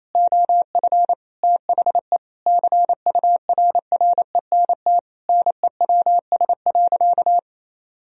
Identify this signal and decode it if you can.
Morse code, heard as a keyed tone, about 28 wpm, 700 Hz. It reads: OF THE CURRENT NEWS.
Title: Sentences from Top 1000 Words - Review Part 4 - 28wpm